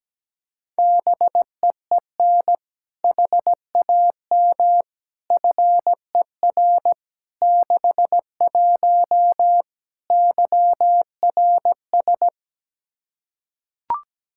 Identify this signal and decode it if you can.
Morse code, heard as a keyed tone, about 17 wpm, 700 Hz. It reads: BEEN HAM FER 61 YRS
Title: K1USN SST QSO Practice 17wpm